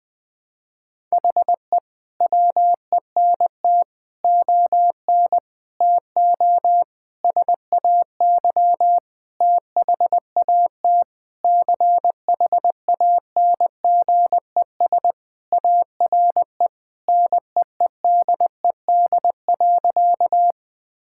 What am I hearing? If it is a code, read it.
Morse code, heard as a keyed tone, about 20 wpm, 700 Hz. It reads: HE WENT ON TO SAY THAT CHANGES ARE NEEDED.
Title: Sentences with 6-Word Phrases - Spoken Beginning - 20wpm